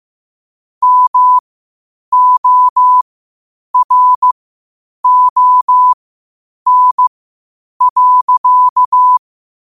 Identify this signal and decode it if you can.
Morse code, heard as a keyed tone, about 15 wpm, spaced as for 8 wpm, 1000 Hz. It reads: MORON.
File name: qotdfiles / 8wpm